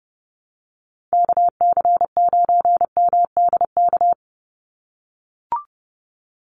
Morse code, heard as a keyed tone, about 30 wpm, 700 Hz. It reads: KC9MDK
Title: USA Call Signs 30wpm